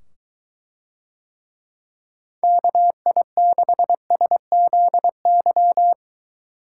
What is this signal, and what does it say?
Morse code, heard as a keyed tone, about 23 wpm, 700 Hz. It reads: KI6SZY